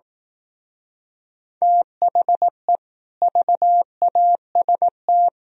Morse code, heard as a keyed tone, about 18 wpm, 700 Hz. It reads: THE VAST
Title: Speed-Racing - Sets of 4 Words 15wpm